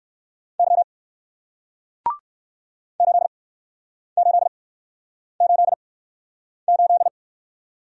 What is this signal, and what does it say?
Morse code, heard as a keyed tone, about 64 wpm, spaced as for 9 wpm, 700 Hz. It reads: Y 8888